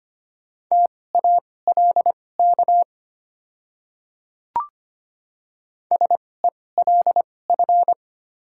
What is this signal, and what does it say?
Morse code, heard as a keyed tone, about 25 wpm, spaced as for 18 wpm, 700 Hz. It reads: TALK SELF
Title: 2x Character Spacing - Top 500 Words - 25wpm